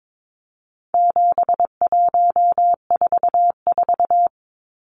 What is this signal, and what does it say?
Morse code, heard as a keyed tone, about 22 wpm, 700 Hz. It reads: 7144
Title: Numbers - 4-Digit 22wpm